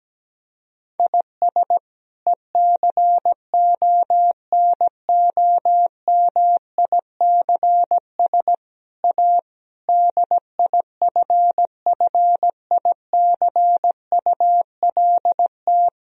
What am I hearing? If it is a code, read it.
Morse code, heard as a keyed tone, about 17 wpm, 700 Hz. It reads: IS ECONOMICS A DIFFICULT
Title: Sentences from Top 1000 Words - Review Part 3 - 17wpm